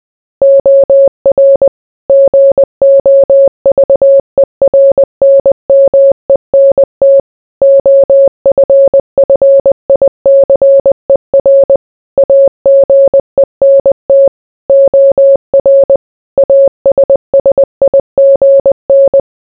Morse code, heard as a keyed tone, about 20 wpm, 550 Hz. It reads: OR GOVERNMENT OFFICER AGENT OR ASSIGN